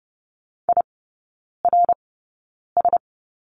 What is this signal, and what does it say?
Morse code, heard as a keyed tone, about 30 wpm, spaced as for 9 wpm, 700 Hz. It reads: IRS
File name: course / 30:10wpm / 087 Lesson 15e - Number 5 - Three Characters 30wpm